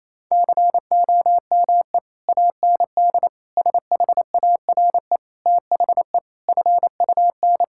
Morse code, heard as a keyed tone, about 27 wpm, 700 Hz. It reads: COME AND SHARE THE FUN
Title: Sentences from Top 1000 Words - Review Part 6 - 28wpm